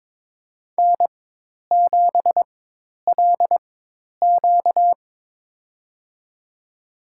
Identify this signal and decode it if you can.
Morse code, heard as a keyed tone, about 22 wpm, spaced as for 10 wpm, 700 Hz. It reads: N7LQ
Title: USA Call Signs - 4 Characters - 4x Character Spacing - 22wpm